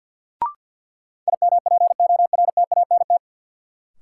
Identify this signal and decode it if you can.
Morse code, heard as a keyed tone, about 47 wpm, 700 Hz. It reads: IMPORTANT
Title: Sentences from Top 600 Words - New Word Drill - 50wpm